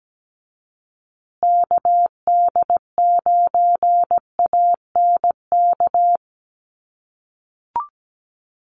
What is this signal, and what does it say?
Morse code, heard as a keyed tone, about 17 wpm, 700 Hz. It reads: KD9ANK
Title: USA Call Signs Repeated 3x 17wpm